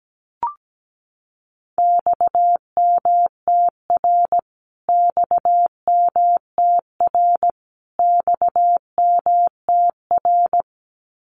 Morse code, heard as a keyed tone, about 17 wpm, 700 Hz. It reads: XMTR XMTR XMTR
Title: Abbreviations Repeated 3x - 17wpm